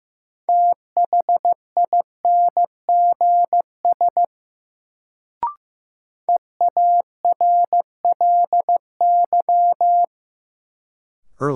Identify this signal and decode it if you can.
Morse code, heard as a keyed tone, about 15 wpm, 700 Hz. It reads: THINGS EARLY
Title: Sets of 3 Words - New Word Drill - 15wpm